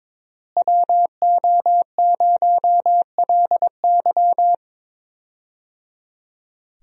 Morse code, heard as a keyed tone, about 22 wpm, 700 Hz. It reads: WO0LY
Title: USA Call Signs - 5 Characters - Repeated 3x - 22wpm